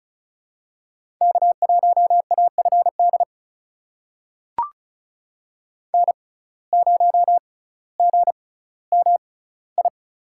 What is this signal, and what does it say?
Morse code, heard as a keyed tone, about 36 wpm, spaced as for 18 wpm, 700 Hz. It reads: K1AFD N0GMI
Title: USA Call Signs - 6x Character Spacing - 35wpm